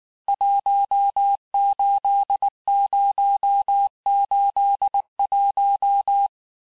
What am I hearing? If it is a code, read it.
Morse code, heard as a keyed tone, about 19 wpm, 800 Hz. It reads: 18081
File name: cours-du-lundi files / cours-1589-texte-FIN-de-QSO-vitesse-18mpm